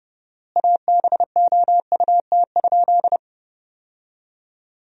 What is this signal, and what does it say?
Morse code, heard as a keyed tone, about 30 wpm, 700 Hz. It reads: ABOUT?
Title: 4x Word-Spacing - Sentences from Top 100 Words - 30wpm